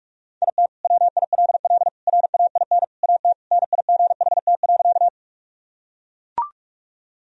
Morse code, heard as a keyed tone, about 42 wpm, 700 Hz. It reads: IT WILL RAIN AT NIGHT.
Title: Sentences from Top 1000 Words - Review Part 1 - 45wpm